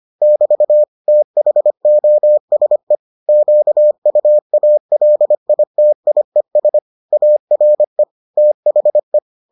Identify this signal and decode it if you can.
Morse code, heard as a keyed tone, about 25 wpm, 600 Hz. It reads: <BT> THOSE QUALITIES ARE THE